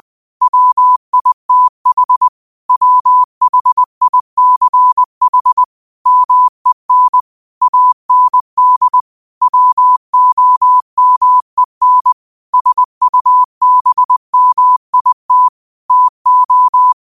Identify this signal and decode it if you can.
Morse code, heard as a keyed tone, about 20 wpm, 1000 Hz. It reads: WITH WHICH MEN AND WOMEN SUBMIT TO
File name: qotdfiles / 20wpm